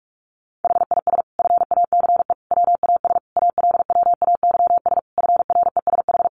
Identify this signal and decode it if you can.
Morse code, heard as a keyed tone, about 44 wpm, 700 Hz. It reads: HIS FACE WAS ALWAYS FRESH
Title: Sentences from Top 1200 Words 45wpm